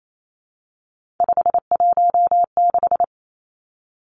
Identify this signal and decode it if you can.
Morse code, heard as a keyed tone, about 28 wpm, 700 Hz. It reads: 516
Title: Numbers - 3-Digit 28wpm